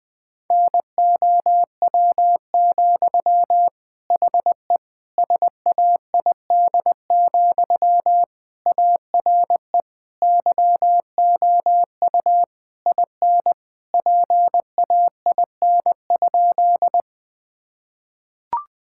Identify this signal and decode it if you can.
Morse code, heard as a keyed tone, about 20 wpm, 700 Hz. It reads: NOW, HE SAID, ARE YOU IN PAIN?
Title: Sentences from Top 1600 Words 20wpm